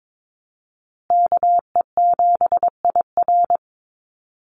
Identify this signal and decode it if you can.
Morse code, heard as a keyed tone, about 22 wpm, 700 Hz. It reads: KE7IR
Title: USA Call Signs Repeated 2x 22wpm